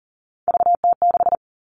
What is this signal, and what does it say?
Morse code, heard as a keyed tone, about 40 wpm, 700 Hz. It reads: VT6
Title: All Cut Numbers - 3-Digit 40wpm